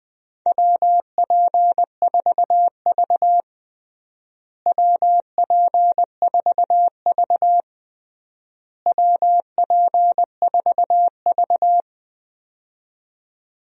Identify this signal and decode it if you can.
Morse code, heard as a keyed tone, about 21 wpm, 700 Hz. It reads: WP4V WP4V WP4V
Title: USA Call Signs - 4 Characters - Repeated 3x - 20wpm